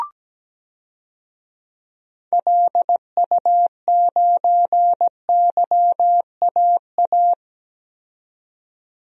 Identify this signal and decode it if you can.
Morse code, heard as a keyed tone, about 17 wpm, 700 Hz. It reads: LU9YAA
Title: DXCC-100 Call Signs - 6 Characters - 17wpm